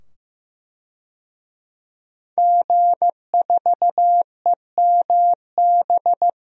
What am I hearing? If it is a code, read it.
Morse code, heard as a keyed tone, about 16 wpm, 700 Hz. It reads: G4EMB
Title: DXCC-100 Call Signs - 5 Characters - 15wpm